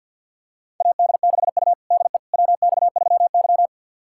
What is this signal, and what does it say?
Morse code, heard as a keyed tone, about 52 wpm, 700 Hz. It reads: AD6U DE WX3Y
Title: USA Callsign DE Callsign - 4 Characters - 50wpm